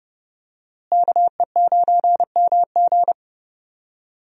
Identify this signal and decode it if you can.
Morse code, heard as a keyed tone, about 30 wpm, 700 Hz. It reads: KE9MG
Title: USA Call Signs - 5 Characters - Repeated 3x - 30wpm